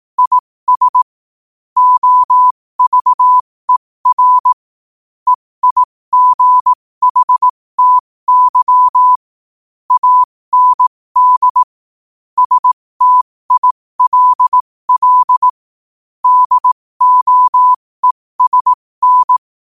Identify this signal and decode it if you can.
Morse code, heard as a keyed tone, about 18 wpm, spaced as for 15 wpm, 1000 Hz. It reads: IS OVER EIGHTY AND STILL DOESN
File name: qotdfiles / 15wpm